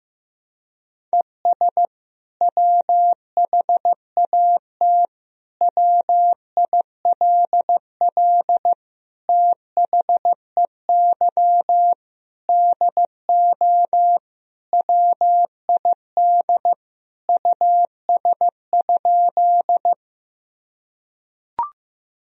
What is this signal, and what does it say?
Morse code, heard as a keyed tone, about 15 wpm, 700 Hz. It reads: ES WHAT WILL THEY DO WID US?
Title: Sentences from Top 200 Words and Abbreviations 15wpm